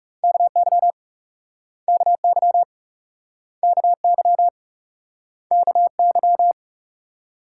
Morse code, heard as a keyed tone, about 41 wpm, 700 Hz. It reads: KY KY KY KY